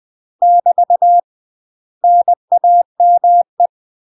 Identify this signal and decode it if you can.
Morse code, heard as a keyed tone, about 20 wpm, 700 Hz. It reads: <BT> NAME
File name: UK MASTER QSOs TRAINING / 20WPM / UK MASTER QSOs 0050